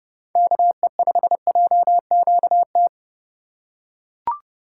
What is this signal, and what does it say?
Morse code, heard as a keyed tone, about 30 wpm, 700 Hz. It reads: KE5JQT
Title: USA Call Signs - 6 Characters - 3x Character Spacing - 30wpm